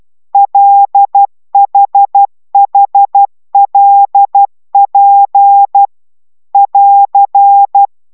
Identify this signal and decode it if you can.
Morse code, heard as a keyed tone, about 12 wpm, 800 Hz. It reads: LHHLP <AR>
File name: taviro tanulas / cw 69 10 10